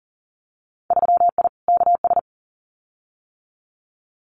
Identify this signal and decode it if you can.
Morse code, heard as a keyed tone, about 40 wpm, 700 Hz. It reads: 3I KS
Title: Winter Field Day Exchanges 40wpm